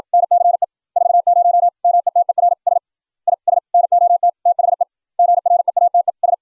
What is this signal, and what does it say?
Morse code, heard as a keyed tone, about 56 wpm, 700 Hz. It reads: NCE 40 METERS IS NOT THE GREATES